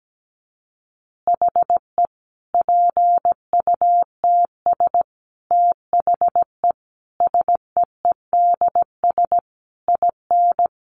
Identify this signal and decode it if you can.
Morse code, heard as a keyed tone, about 17 wpm, 700 Hz. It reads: HE PUTS THE SEEDS IN
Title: Easy Phrases 17wpm